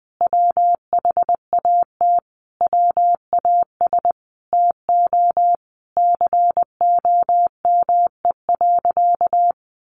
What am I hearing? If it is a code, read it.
Morse code, heard as a keyed tone, about 20 wpm, 700 Hz. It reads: WHAT WAS TO COME.
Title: Sentences from Top 700 Words 20wpm